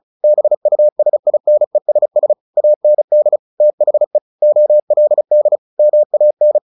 Morse code, heard as a keyed tone, about 34 wpm, 600 Hz. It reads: BUSINESS AND THE OLD MAN
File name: the old man and the sea 2 35wpm